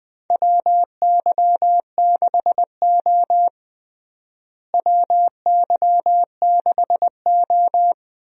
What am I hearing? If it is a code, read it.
Morse code, heard as a keyed tone, about 21 wpm, 700 Hz. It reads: WY6O WY6O
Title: USA Call Signs - 4 Characters - Repeated 2x - 20wpm